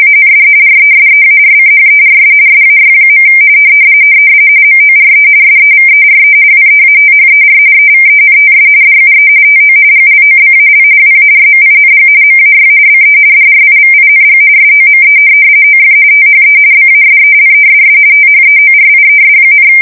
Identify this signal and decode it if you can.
RTTY, about 45.45 baud. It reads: AND RESEARCHERS SAY. MORE SEVERE WEATHER EVENTS DUE TO GLOBAL WARMING MAY BE DRIVING SOME OF THE BASELESS THEORIES,
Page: 170 Shift 2125/2295